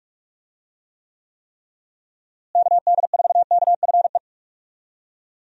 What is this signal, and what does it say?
Morse code, heard as a keyed tone, about 46 wpm, 700 Hz. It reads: KD4KFE
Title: USA Call Signs - 6 Characters - Repeated 3x - 45wpm